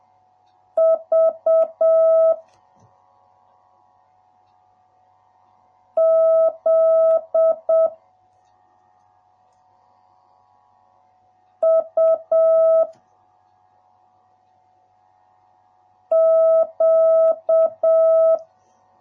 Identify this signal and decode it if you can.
Morse code, heard as a keyed tone, about 7 wpm, 650 Hz. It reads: V Z U Q